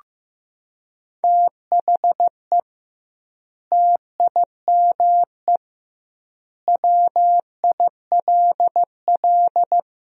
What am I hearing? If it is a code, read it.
Morse code, heard as a keyed tone, about 15 wpm, 700 Hz. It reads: THE TIME WILL
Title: 2x Word-Spacing - Sentences from Top 100 Words - 15wpm